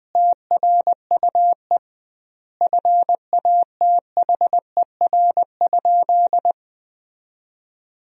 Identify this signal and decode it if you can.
Morse code, heard as a keyed tone, about 20 wpm, 700 Hz. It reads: TRUE FATHER?
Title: 2x Word-Spacing - Sentences from Top 400 Words - 20wpm